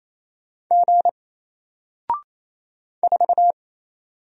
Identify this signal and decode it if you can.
Morse code, heard as a keyed tone, about 28 wpm, 700 Hz. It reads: G 4
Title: Single Letter-Number Alt 28wpm